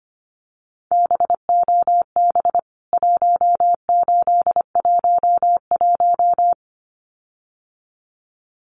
Morse code, heard as a keyed tone, about 25 wpm, 700 Hz. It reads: BOB 1811
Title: CWT Exchange 25wpm